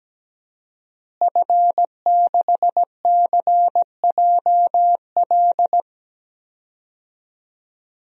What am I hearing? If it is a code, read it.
Morse code, heard as a keyed tone, about 17 wpm, 700 Hz. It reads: F6CJL
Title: DXCC-100 Call Signs - 5 Characters - Repeated 3x - 17wpm